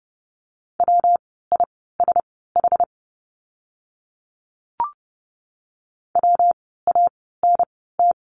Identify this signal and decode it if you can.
Morse code, heard as a keyed tone, about 30 wpm, spaced as for 17 wpm, 700 Hz. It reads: WISH WANT
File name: course / 30:10wpm / 115 Lesson 19f - Letter w - Words 30wpm